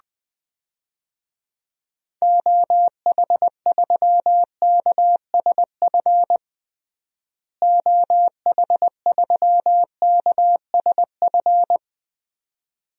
Morse code, heard as a keyed tone, about 21 wpm, 700 Hz. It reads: OH3KSF OH3KSF
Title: DXCC-100 Call Signs - 6 Characters - Repeated 2x - 20wpm